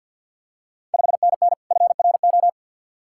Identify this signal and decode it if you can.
Morse code, heard as a keyed tone, about 53 wpm, 700 Hz. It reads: 5NN FRO
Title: Ontario QSO Party Exchange 50wpm